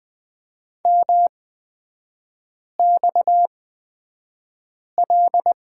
Morse code, heard as a keyed tone, about 20 wpm, spaced as for 5 wpm, 700 Hz. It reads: MXL
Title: 243 Lesson 39e - Letter z - Three Characters 20wpm